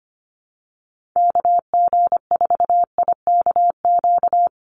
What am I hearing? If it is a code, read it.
Morse code, heard as a keyed tone, about 25 wpm, 700 Hz. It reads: KG4IKQ
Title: Call Signs Repeated 2x - 25wpm